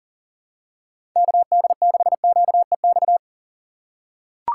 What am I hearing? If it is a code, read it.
Morse code, heard as a keyed tone, about 40 wpm, 700 Hz. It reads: KD6QEX
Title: USA Call Signs - 6x Character Spacing - 40wpm